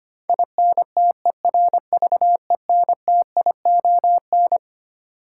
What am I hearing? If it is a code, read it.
Morse code, heard as a keyed tone, about 25 wpm, 700 Hz. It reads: INTERVENTION
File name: Twelve Letter Words - 25wpm